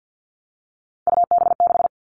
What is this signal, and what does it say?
Morse code, heard as a keyed tone, about 50 wpm, 700 Hz. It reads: UB6
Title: All Cut Numbers - 3-Digit 50wpm